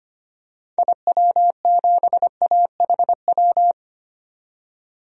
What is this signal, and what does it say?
Morse code, heard as a keyed tone, about 25 wpm, 700 Hz. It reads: IW7AHW
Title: DXCC-100 Call Signs - 6 Characters - 25wpm